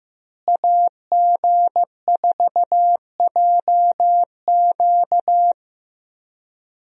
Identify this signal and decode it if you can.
Morse code, heard as a keyed tone, about 15 wpm, 700 Hz. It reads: AG4JQ
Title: USA Call Signs - 5 Characters - Repeated 2x - 15wpm